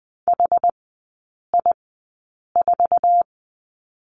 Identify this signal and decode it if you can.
Morse code, heard as a keyed tone, about 20 wpm, spaced as for 8 wpm, 700 Hz. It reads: HI4
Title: 067 Lesson 12e - Letter d - Three Characters 20wpm